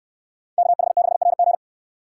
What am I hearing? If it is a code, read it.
Morse code, heard as a keyed tone, about 69 wpm, 700 Hz. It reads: KH6RG